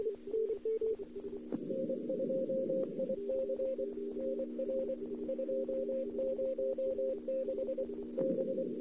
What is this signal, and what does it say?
Morse code, heard as a keyed tone, about 24 wpm, 530 Hz. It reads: N2IC NR 206 B
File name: CW / 2008-ss-cw-k5tr-op-n5rz -02:00